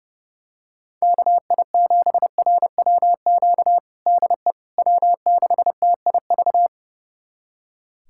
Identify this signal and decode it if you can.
Morse code, heard as a keyed tone, about 30 wpm, 700 Hz. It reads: KI7RWQ DE W6TIV
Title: USA Callsign DE Callsign 30wpm